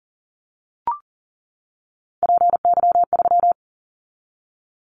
Morse code, heard as a keyed tone, about 40 wpm, 700 Hz. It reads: PY3
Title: Callsign Prefixes for Top 50 DXCC - 40wpm